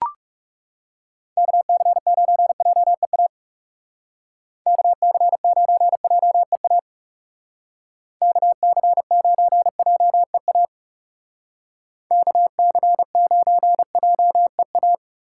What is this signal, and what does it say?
Morse code, heard as a keyed tone, about 40 wpm, 700 Hz. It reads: KC9JEA KC9JEA KC9JEA KC9JEA